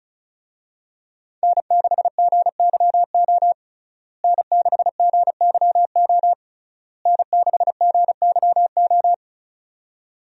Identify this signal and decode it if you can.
Morse code, heard as a keyed tone, about 34 wpm, 700 Hz. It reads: N6GYO N6GYO N6GYO
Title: USA Call Signs - 5 Characters - Repeated 3x - 35wpm